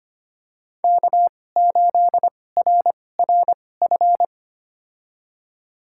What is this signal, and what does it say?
Morse code, heard as a keyed tone, about 25 wpm, spaced as for 18 wpm, 700 Hz. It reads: K8RRF
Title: USA Call Signs - 5 Characters - 2x Character Spacing - 25wpm